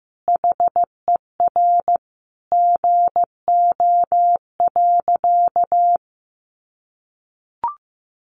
Morse code, heard as a keyed tone, about 15 wpm, 700 Hz. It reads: HER GO.
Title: Sentences from Top 1500 Words 15wpm